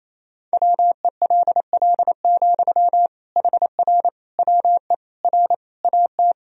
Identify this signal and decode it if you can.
Morse code, heard as a keyed tone, about 28 wpm, 700 Hz. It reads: WELL, HR WE R AT
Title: Sentences from Top 300 Words and Abbreviations 28wpm